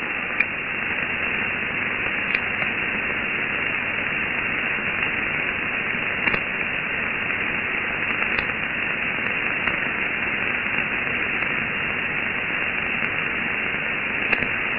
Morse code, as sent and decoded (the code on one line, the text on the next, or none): none